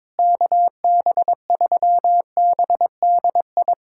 Morse code, heard as a keyed tone, about 22 wpm, 700 Hz. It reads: KB3BDI
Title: USA Callsign DE Callsign 22wpm